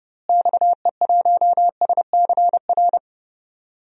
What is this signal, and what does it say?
Morse code, heard as a keyed tone, about 31 wpm, 700 Hz. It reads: XE1SCR
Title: DXCC-100 Call Signs - 6 Characters - Repeated 3x - 30wpm